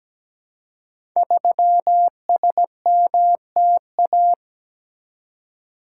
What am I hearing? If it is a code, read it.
Morse code, heard as a keyed tone, about 17 wpm, 700 Hz. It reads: 3SMTA
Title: Code Groups - 5 Letter:Numbers - 17wpm